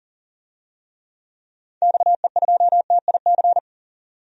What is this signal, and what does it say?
Morse code, heard as a keyed tone, about 37 wpm, 700 Hz. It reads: XE2TIC
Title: DXCC-100 Call Signs - 6 Characters - 40wpm